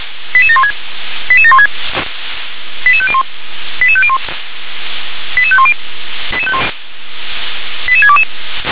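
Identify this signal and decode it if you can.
Five-tone selective call (ZVEI), heard as quick runs of five tones, tones of about 70 ms. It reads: 88716 88716 88501 88501 88510 88622
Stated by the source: Five-tone ZVEI selective call received by SDR.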